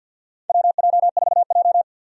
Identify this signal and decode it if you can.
Morse code, heard as a keyed tone, about 48 wpm, 700 Hz. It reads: WJ4J
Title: USA Call Signs - 4 Characters - 50wpm